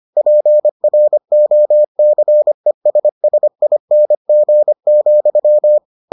Morse code, heard as a keyed tone, about 25 wpm, 600 Hz. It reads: PROCESSING,